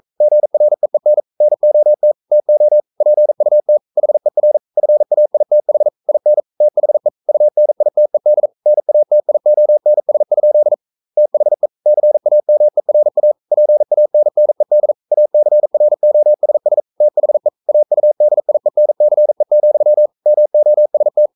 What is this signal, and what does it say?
Morse code, heard as a keyed tone, about 42 wpm, 600 Hz. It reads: GREEN NOT TO PUT HER FAITH IN THE UNITED NATIONS? THE CAMERA PANNED ACROSS THE AUDIENCE, MOST
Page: Romanen Never af Ken Follet på engelsk.